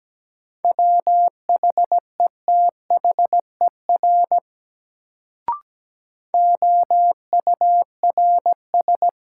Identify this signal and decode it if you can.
Morse code, heard as a keyed tone, about 17 wpm, 700 Hz. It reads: WHETHER OURS
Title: Sentences from Top 1300 Words - New Word Drill Alt - 17wpm